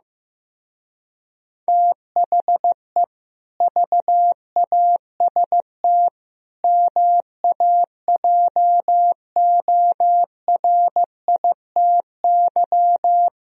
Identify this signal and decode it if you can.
Morse code, heard as a keyed tone, about 15 wpm, 700 Hz. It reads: THE VAST MAJORITY
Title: Speed-Racing - Sets of 4 Words 15wpm